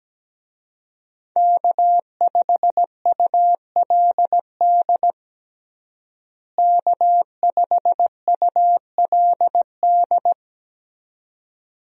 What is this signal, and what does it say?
Morse code, heard as a keyed tone, about 17 wpm, 700 Hz. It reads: K5ULD K5ULD
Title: USA Call Signs - 5 Characters - Repeated 2x - 17wpm